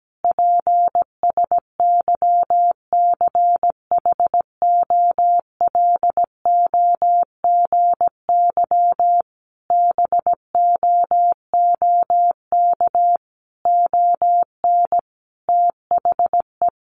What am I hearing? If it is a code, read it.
Morse code, heard as a keyed tone, about 17 wpm, 700 Hz. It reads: PSYCHOLOGY BOOK ON THE
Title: Sentences from Top 1400 Words 17wpm